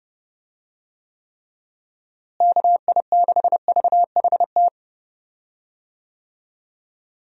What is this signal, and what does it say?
Morse code, heard as a keyed tone, about 30 wpm, 700 Hz. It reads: KI6VHT
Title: USA Call Signs - 6 Characters - Repeated 3x - 30wpm